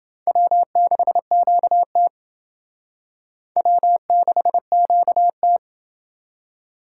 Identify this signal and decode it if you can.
Morse code, heard as a keyed tone, about 28 wpm, 700 Hz. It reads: W6QT W6QT